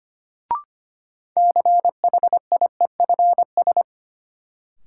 Morse code, heard as a keyed tone, about 25 wpm, 700 Hz. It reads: CHIEFS
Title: Sets of 5 Words Encore - New Word Drill - 25wpm